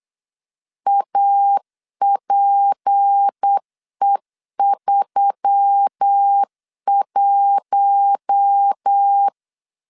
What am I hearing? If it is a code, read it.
Morse code, heard as a keyed tone, about 8 wpm, 800 Hz. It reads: APE31
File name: LEZIONI / 6 PKX0 / 6 P 8wpm 3025